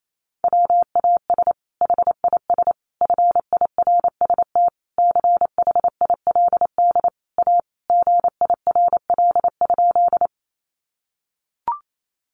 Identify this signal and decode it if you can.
Morse code, heard as a keyed tone, about 28 wpm, 700 Hz. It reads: WAS HIS FIRST CHILD A GIRL?
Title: Sentences from Top 800 Words 28wpm